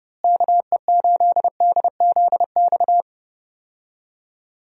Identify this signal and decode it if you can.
Morse code, heard as a keyed tone, about 32 wpm, 700 Hz. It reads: KE8DZX
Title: USA Call Signs - 6 Characters - Repeated 3x - 30wpm